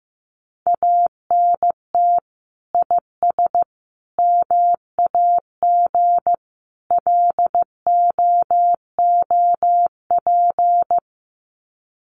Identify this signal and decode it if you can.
Morse code, heard as a keyed tone, about 15 wpm, 700 Hz. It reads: ANT IS MAG LOOP
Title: Common QSO Phrases 15wpm